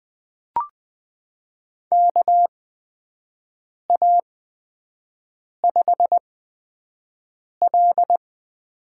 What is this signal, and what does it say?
Morse code, heard as a keyed tone, about 20 wpm, spaced as for 5 wpm, 700 Hz. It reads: KA5L
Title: USA Call Signs - 6 Characters - 8x Character Spacing - 20wpm